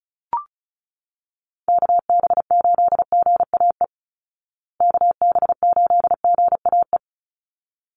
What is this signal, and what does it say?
Morse code, heard as a keyed tone, about 34 wpm, 700 Hz. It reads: KB8GAE KB8GAE
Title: Call Signs Repeated 3x - 35wpm